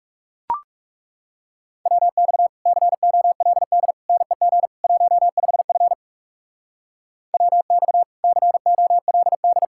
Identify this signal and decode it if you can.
Morse code, heard as a keyed tone, about 43 wpm, 700 Hz. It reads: WX COLD NEG 15F WX COLD